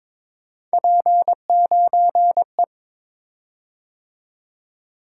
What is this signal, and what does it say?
Morse code, heard as a keyed tone, about 22 wpm, 700 Hz. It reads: P9E
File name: practice / 22wpm / Three Letter-Number Combinations - 22wpm